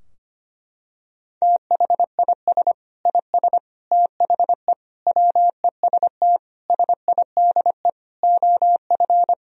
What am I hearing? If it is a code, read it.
Morse code, heard as a keyed tone, about 25 wpm, 700 Hz. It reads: THIS IS THE WEST SIDE OF
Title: Sentences from Top 1000 Words - Review Part 2 - 25wpm